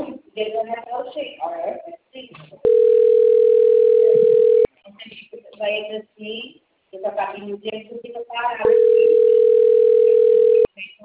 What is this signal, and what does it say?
Ringback tone (North America), 440 + 480 Hz, with speech.
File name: out- - - - - .51231